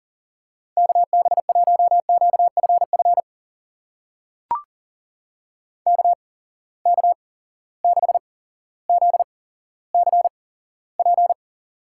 Morse code, heard as a keyed tone, about 39 wpm, spaced as for 17 wpm, 700 Hz. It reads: KB1QFF KK6ZCP
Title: USA Call Signs - 6 Characters - 8x Character Spacing - 40wpm